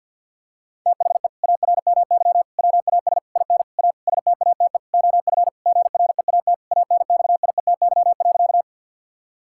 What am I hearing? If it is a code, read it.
Morse code, heard as a keyed tone, about 50 wpm, 700 Hz. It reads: THE ARMY WAS IN A STATE OF GREAT ANXIETY.
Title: Sentences from Top 1600 Words 50wpm